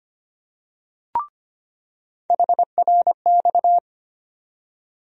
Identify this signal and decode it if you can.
Morse code, heard as a keyed tone, about 25 wpm, 700 Hz. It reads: HRX
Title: USA Call Sign Postfixes - 3 Letters 25wpm